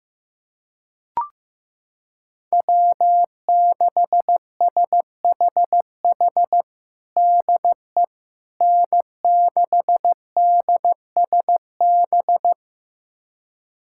Morse code, heard as a keyed tone, about 15 wpm, 700 Hz. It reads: W6SHH DE N6DSB
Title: USA Callsign DE Callsign - 5 Characters - 15wpm